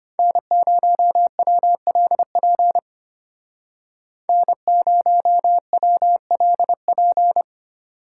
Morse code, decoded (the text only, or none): N0WLP N0WLP